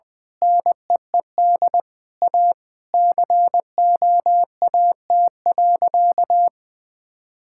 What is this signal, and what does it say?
Morse code, heard as a keyed tone, about 20 wpm, 700 Hz. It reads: NEED A COAT.
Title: Sentences with 4-Word Phrases - 20wpm